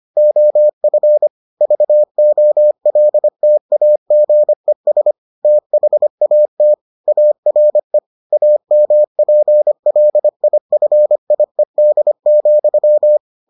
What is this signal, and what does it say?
Morse code, heard as a keyed tone, about 25 wpm, 600 Hz. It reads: OF VOLTAGES THAT ARE AMPLIFIED,